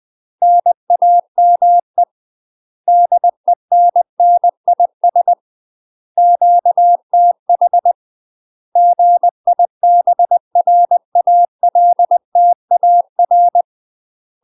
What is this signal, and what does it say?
Morse code, heard as a keyed tone, about 20 wpm, 700 Hz. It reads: NAME DENNIS QTH GIBRALTAR
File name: UK MASTER QSOs TRAINING/20WPM